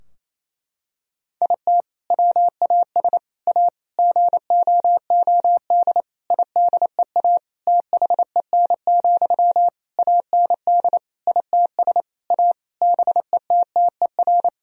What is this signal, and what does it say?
Morse code, heard as a keyed tone, about 28 wpm, 700 Hz. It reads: IT WAS A GOOD IDEA THEN, AND ITS A BETTER
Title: Sentences from Top 1000 Words - Review Part 4 - 28wpm